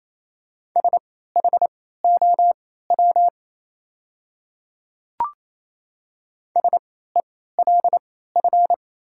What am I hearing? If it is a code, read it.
Morse code, heard as a keyed tone, about 27 wpm, spaced as for 16 wpm, 690 Hz. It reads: SHOW SELF
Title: 3x Character Spacing - Top 500 Words - 28wpm